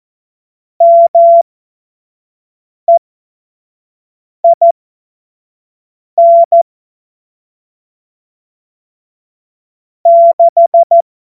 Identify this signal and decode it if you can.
Morse code, heard as a keyed tone, about 14 wpm, spaced as for 5 wpm, 680 Hz. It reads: MEIN 6
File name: Morsekurs / Witze / Witz 11. Mai 2021, 25 F70